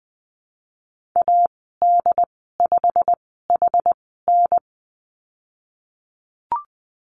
Morse code, heard as a keyed tone, about 20 wpm, spaced as for 15 wpm, 700 Hz. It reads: AD5HN
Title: 088 Lesson 15g - Number 5 - Call Signs 20wpm